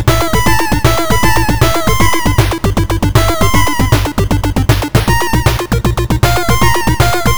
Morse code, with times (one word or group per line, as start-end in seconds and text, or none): none